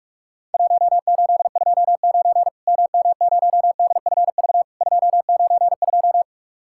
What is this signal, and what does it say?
Morse code, heard as a keyed tone, about 43 wpm, 700 Hz. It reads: 1829 MM0DFV 292
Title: Worked All Europe DX Contest - QTC Exchange - 45wpm